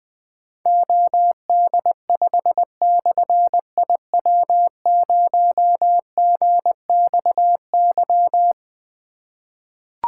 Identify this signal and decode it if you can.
Morse code, heard as a keyed tone, about 20 wpm, 700 Hz. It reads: OD5/IW0GXY
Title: Difficult Call Signs 20wpm